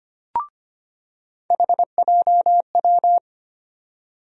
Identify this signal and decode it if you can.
Morse code, heard as a keyed tone, about 25 wpm, 700 Hz. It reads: HJW
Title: USA Call Sign Postfixes - 3 Letters 25wpm